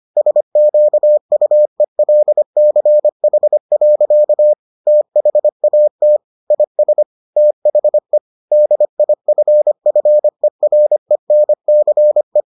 Morse code, heard as a keyed tone, about 25 wpm, 600 Hz. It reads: SQUELCH. THAT IS THE DIFFERENCE